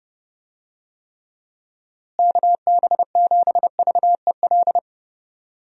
Morse code, heard as a keyed tone, about 29 wpm, 700 Hz. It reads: KB7VEL
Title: USA Call Signs - 6 Characters - Repeated 3x - 30wpm